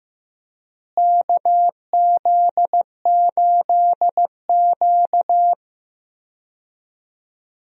Morse code, heard as a keyed tone, about 15 wpm, 700 Hz. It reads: KZ8Q
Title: USA Call Signs - 4 Characters - Repeated 2x - 15wpm